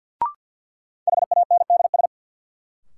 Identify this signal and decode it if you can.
Morse code, heard as a keyed tone, about 52 wpm, 700 Hz. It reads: HANDS
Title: Sentences from Top 1000 Words - Review Part 6 - New Word Drill - 50wpm